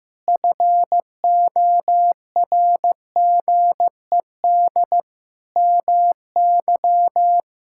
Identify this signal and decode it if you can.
Morse code, heard as a keyed tone, about 15 wpm, 700 Hz. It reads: FORGED MY
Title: Pangram Challenge v2 - 15wpm